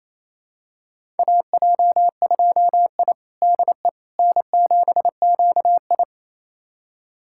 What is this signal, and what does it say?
Morse code, heard as a keyed tone, about 28 wpm, 700 Hz. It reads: AJ2I DE N7QI
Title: USA Callsign DE Callsign - 4 Characters - 28wpm